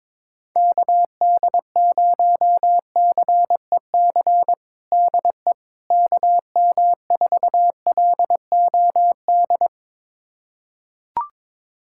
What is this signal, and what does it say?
Morse code, heard as a keyed tone, about 21 wpm, 700 Hz. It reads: KD0CEC DE KM4LOD
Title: USA Callsign DE Callsign - 6 Characters - 22wpm